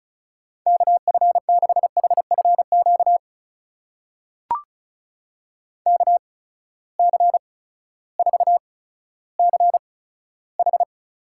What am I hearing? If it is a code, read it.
Morse code, heard as a keyed tone, about 36 wpm, spaced as for 16 wpm, 700 Hz. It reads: KF6HFQ KC4CH
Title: USA Call Signs - 6 Characters - 8x Character Spacing - 35wpm